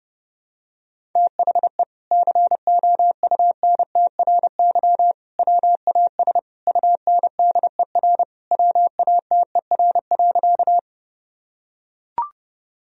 Morse code, heard as a keyed tone, about 32 wpm, 700 Hz. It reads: THE COUNTRY WAS UNDER WATER.
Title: Sentences from Top 1000 Words - Review Part 1 - 30wpm